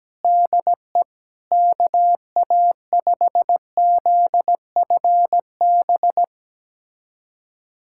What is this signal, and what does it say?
Morse code, heard as a keyed tone, about 17 wpm, 700 Hz. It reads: DE KA5ZFB
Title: USA Callsign DE Callsign - 6 Characters - 17wpm